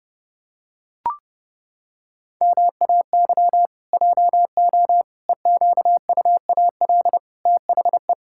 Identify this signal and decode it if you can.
Morse code, heard as a keyed tone, about 30 wpm, 700 Hz. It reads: MAY JO EQUAL THE
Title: Pangram Challenge v2 - 30wpm